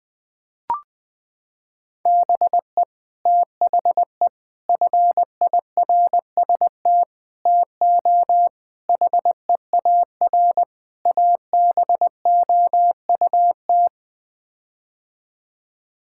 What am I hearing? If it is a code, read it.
Morse code, heard as a keyed tone, about 21 wpm, 700 Hz. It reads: BE THE FIRST TO HEAR ABOUT
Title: Sets of 6 Words - 20wpm